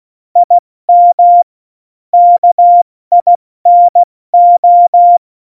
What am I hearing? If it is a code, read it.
Morse code, heard as a keyed tone, about 16 wpm, 700 Hz. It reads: IM KINO
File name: Morsekurs / Witze / Witz 19. Juni 2018, 70 F80